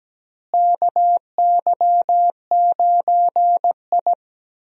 Morse code, heard as a keyed tone, about 17 wpm, 700 Hz. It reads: KY9I
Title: USA Call Signs - 4 Characters - Repeated 3x - 17wpm